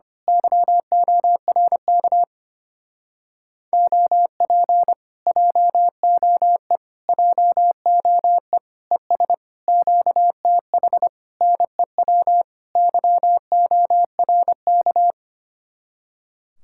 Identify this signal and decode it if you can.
Morse code, heard as a keyed tone, about 24 wpm, 700 Hz. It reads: YORK OP JOE JOE ES QTH NEW YORK
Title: Speed-Racing Common QSO Phrases - 25wpm